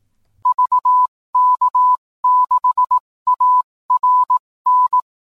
Morse code, heard as a keyed tone, about 18 wpm, spaced as for 15 wpm, 1000 Hz. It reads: VK6ARN
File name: newswest